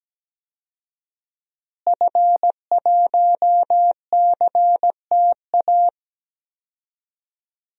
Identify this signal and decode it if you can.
Morse code, heard as a keyed tone, about 17 wpm, 700 Hz. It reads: F1CTA
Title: DXCC-100 Call Signs - 5 Characters - Repeated 2x - 17wpm